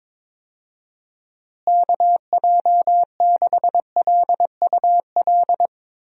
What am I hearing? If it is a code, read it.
Morse code, heard as a keyed tone, about 22 wpm, 700 Hz. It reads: KJ6LUL
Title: USA Call Signs - 6 Characters - Repeated 2x - 22wpm